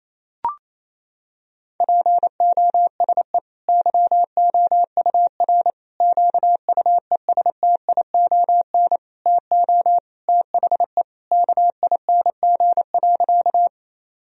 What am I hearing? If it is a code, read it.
Morse code, heard as a keyed tone, about 27 wpm, 700 Hz. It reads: POSE YOUR QUESTION TO THE KING.
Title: Sentences from Top 1000 Words - Review Part 2 - 28wpm